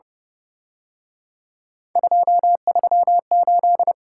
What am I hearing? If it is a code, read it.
Morse code, heard as a keyed tone, about 30 wpm, 700 Hz. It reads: 238
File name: Speed-Racing - Three Letter-Number Combinations - 25wpm